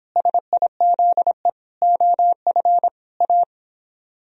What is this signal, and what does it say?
Morse code, heard as a keyed tone, about 26 wpm, 700 Hz. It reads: SIZE OF A